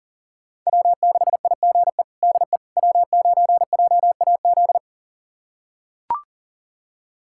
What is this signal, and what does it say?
Morse code, heard as a keyed tone, about 37 wpm, 700 Hz. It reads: W6IGE DE W9JAZ
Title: USA Callsign DE Callsign - 5 Characters - 40wpm